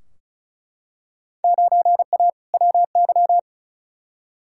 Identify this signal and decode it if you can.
Morse code, heard as a keyed tone, about 36 wpm, 700 Hz. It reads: 9A WY
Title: Field Day Exchanges Encore - 35wpm